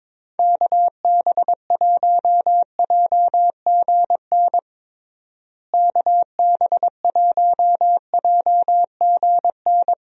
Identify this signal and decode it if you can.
Morse code, heard as a keyed tone, about 22 wpm, 700 Hz. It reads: KB1JGN KB1JGN
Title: USA Call Signs - 6 Characters - Repeated 3x - 22wpm